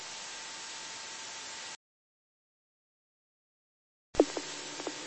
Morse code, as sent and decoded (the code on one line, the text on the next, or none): none